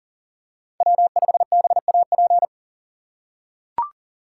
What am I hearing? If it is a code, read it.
Morse code, heard as a keyed tone, about 40 wpm, 700 Hz. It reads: W5BAP
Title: USA Call Signs - 5 Characters - 4x Character Spacing - 40wpm